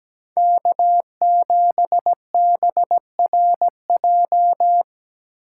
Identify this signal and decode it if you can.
Morse code, heard as a keyed tone, about 17 wpm, 700 Hz. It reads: K7BRJ
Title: USA Call Signs - 5 Characters - 6x Character Spacing - 17wpm